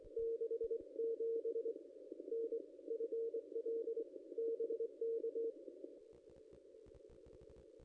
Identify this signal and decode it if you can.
Morse code, heard as a keyed tone, about 23 wpm, 460 Hz. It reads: 67 NFL BK
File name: contests / ss-cw-k5tr-op-n5rz -1800